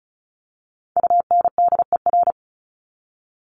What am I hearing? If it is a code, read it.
Morse code, heard as a keyed tone, about 34 wpm, 700 Hz. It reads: UNDER
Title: Top 200 Words 35wpm